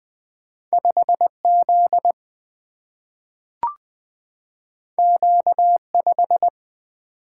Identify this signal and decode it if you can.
Morse code, heard as a keyed tone, about 20 wpm, 700 Hz. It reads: 5 Z Q 5
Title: Single-Letter Single-Number 20wpm